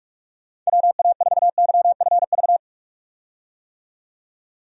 Morse code, heard as a keyed tone, about 46 wpm, 700 Hz. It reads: WA4YFV
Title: USA Call Signs - 6 Characters - 45wpm